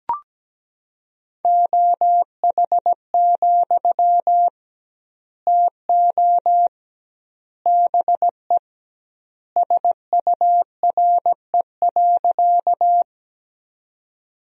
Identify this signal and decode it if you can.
Morse code, heard as a keyed tone, about 17 wpm, 700 Hz. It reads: OH, TO BE SURE.
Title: 2x Word-Spacing - Sentences from Top 500 Words - 17wpm